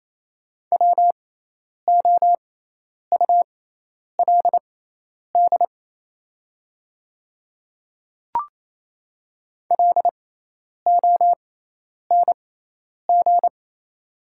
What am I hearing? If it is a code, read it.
Morse code, heard as a keyed tone, about 28 wpm, spaced as for 10 wpm, 700 Hz. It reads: WOULD LONG
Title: 6x Character Spacing - Top 100 Words - 28wpm